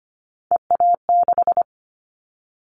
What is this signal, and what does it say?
Morse code, heard as a keyed tone, about 25 wpm, 700 Hz. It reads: EA6
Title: Callsign Prefixes for Top 50 DXCC - 25wpm